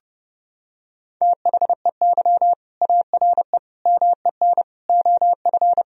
Morse code, heard as a keyed tone, about 32 wpm, 700 Hz. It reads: THEY ARE MEN OF
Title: Sentences from Top 1000 Words - Review Part 2 - 30wpm